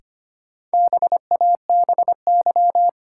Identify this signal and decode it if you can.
Morse code, heard as a keyed tone, about 25 wpm, 700 Hz. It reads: BABY
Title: Sentences from Top 1300 Words - New Word Drill Alt - 25wpm